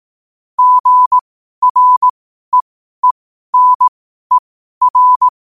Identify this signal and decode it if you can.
Morse code, heard as a keyed tone, about 18 wpm, spaced as for 12 wpm, 1000 Hz. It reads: GREENER
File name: qotdfiles / 12wpm